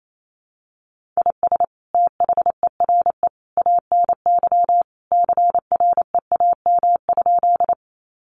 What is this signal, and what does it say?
Morse code, heard as a keyed tone, about 28 wpm, 700 Hz. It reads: IS THERE ANY CREAM?
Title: Sentences from Top 1500 Words 28wpm